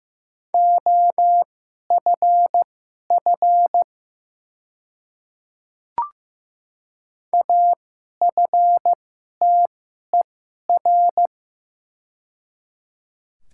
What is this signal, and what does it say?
Morse code, heard as a keyed tone, about 15 wpm, spaced as for 11 wpm, 700 Hz. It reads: OFF AFTER
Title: 2x Character Spacing - Top 500 Words - 15wpm